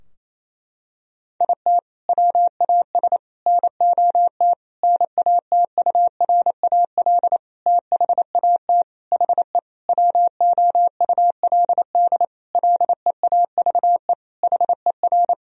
Morse code, heard as a keyed tone, about 28 wpm, 700 Hz. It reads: IT WAS NOT NATURAL THAT HE WOULD LEAVE HER
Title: Sentences from Top 1000 Words - Review Part 4 - 28wpm